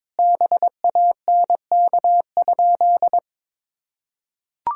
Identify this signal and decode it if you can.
Morse code, heard as a keyed tone, about 22 wpm, 700 Hz. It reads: BANK?
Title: Sentences from Top 1000 Words - Review Part 6 - 22wpm